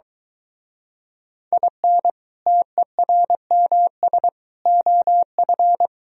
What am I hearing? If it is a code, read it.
Morse code, heard as a keyed tone, about 23 wpm, 700 Hz. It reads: IN TERMS OF